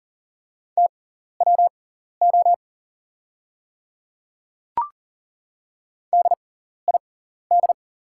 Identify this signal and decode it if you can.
Morse code, heard as a keyed tone, about 40 wpm, spaced as for 14 wpm, 690 Hz. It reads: TWO DID
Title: 6x Character Spacing - Top 100 Words - 40wpm